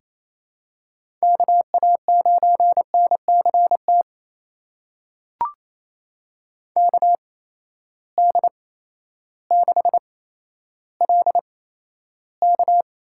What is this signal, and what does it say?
Morse code, heard as a keyed tone, about 28 wpm, spaced as for 13 wpm, 700 Hz. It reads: KA9NCT KD6LK
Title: USA Call Signs - 8x Character Spacing - 28wpm